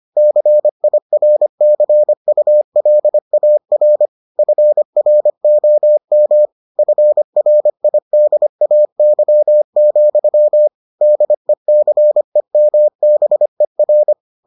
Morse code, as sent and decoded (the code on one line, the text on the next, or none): -.-. .. .-. -.-. ..- .-.. .- .-. / ..-. .-. --- -- / ..-. .-. .. -.. .- -.-- --..-- / -.. -.-. -- -... .-.
CIRCULAR FROM FRIDAY, DECEMBER